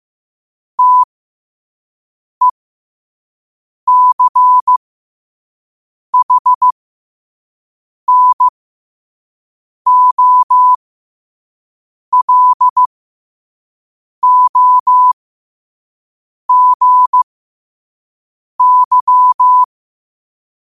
Morse code, heard as a keyed tone, about 15 wpm, spaced as for 5 wpm, 1000 Hz. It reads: TECHNOLOGY
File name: qotdfiles / 5wpm